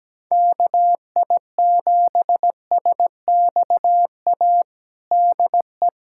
Morse code, heard as a keyed tone, about 17 wpm, 700 Hz. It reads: KI7SXA DE
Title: USA Callsign DE Callsign - 6 Characters - 17wpm